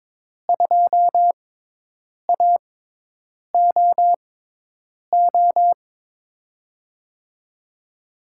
Morse code, heard as a keyed tone, about 22 wpm, spaced as for 8 wpm, 700 Hz. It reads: 2AOO
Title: USA Call Signs - 5 Characters - 6x Character Spacing - 22wpm